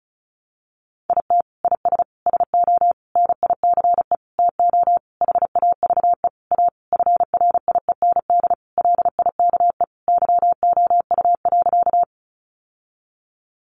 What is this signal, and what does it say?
Morse code, heard as a keyed tone, about 34 wpm, 700 Hz. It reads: IT IS SO NICE TO HAVE A FRIEND LIKE YOU.
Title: Sentences from Top 1100 Words 35wpm